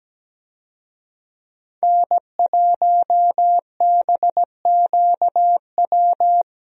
Morse code, heard as a keyed tone, about 17 wpm, 700 Hz. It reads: N1BQW
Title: USA Call Signs - 5 Characters - Repeated 2x - 17wpm